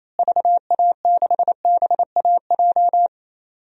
Morse code, heard as a keyed tone, about 28 wpm, 700 Hz. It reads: VA6BAJ
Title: DXCC-100 Call Signs - 6 Characters - Repeated 3x - 28wpm